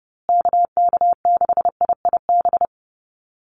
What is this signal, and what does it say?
Morse code, heard as a keyed tone, about 30 wpm, 700 Hz. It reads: KK6IIB
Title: USA Call Signs Repeated 2x 30wpm